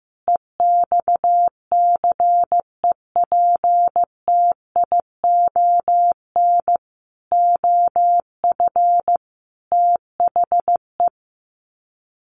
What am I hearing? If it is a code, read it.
Morse code, heard as a keyed tone, about 15 wpm, 700 Hz. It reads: EXCEPTION OF THE
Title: Sets of 5 Words Encore 15wpm